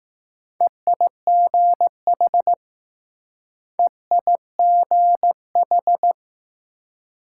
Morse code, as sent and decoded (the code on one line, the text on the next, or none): . .. --. .... / . .. --. ....
EIGH EIGH